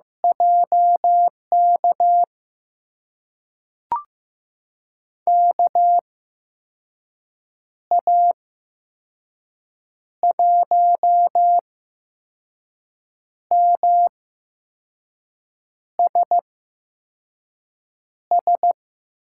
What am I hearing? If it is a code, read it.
Morse code, heard as a keyed tone, about 15 wpm, spaced as for 5 wpm, 700 Hz. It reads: JK KA1MSS
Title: USA Call Signs - 6 Characters - 8x Character Spacing - 15wpm